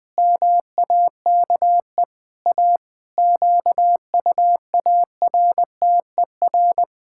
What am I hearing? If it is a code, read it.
Morse code, heard as a keyed tone, about 20 wpm, 700 Hz. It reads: MAKE A QUARTER
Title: Pangram Challenge v2 - 20wpm